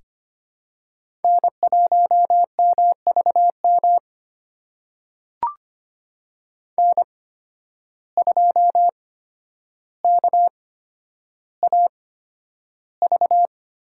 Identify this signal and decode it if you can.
Morse code, heard as a keyed tone, about 25 wpm, spaced as for 11 wpm, 700 Hz. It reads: N1MVM N2KAV
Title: USA Call Signs - 5 Characters - 8x Character Spacing - 25wpm